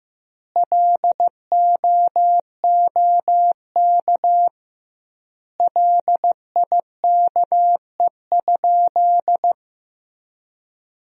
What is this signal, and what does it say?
Morse code, heard as a keyed tone, about 15 wpm, 700 Hz. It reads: LOOK LIKE?
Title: 2x Word-Spacing - Sentences from Top 100 Words - 15wpm